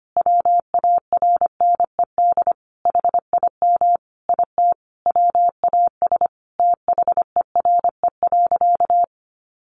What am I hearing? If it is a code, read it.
Morse code, heard as a keyed tone, about 25 wpm, 700 Hz. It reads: WARNED HIM IT WAS THERE.
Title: A Sound Of Thunder 25wpm